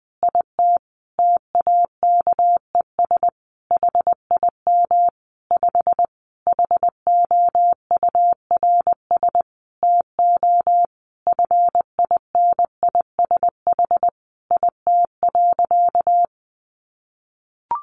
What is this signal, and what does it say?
Morse code, heard as a keyed tone, about 20 wpm, 700 Hz. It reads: IT TAKES HIM 5 HOURS TO FINISH IT.
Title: Additional Easy Phrases 20wpm